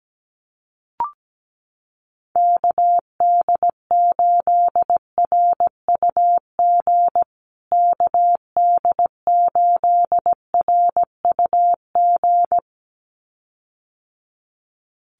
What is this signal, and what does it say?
Morse code, heard as a keyed tone, about 17 wpm, 700 Hz. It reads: KD8RUG KD8RUG
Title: USA Call Signs Repeated 2x 17wpm